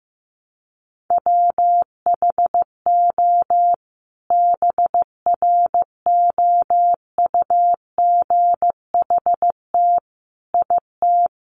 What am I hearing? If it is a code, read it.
Morse code, heard as a keyed tone, about 15 wpm, 700 Hz. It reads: WHO BROUGHT IT
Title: Sentences from Top 600 Words 15wpm